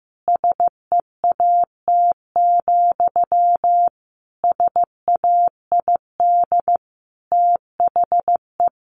Morse code, heard as a keyed tone, about 14 wpm, 700 Hz. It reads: SEAT, SAID THE
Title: Sentences from Top 1500 Words 15wpm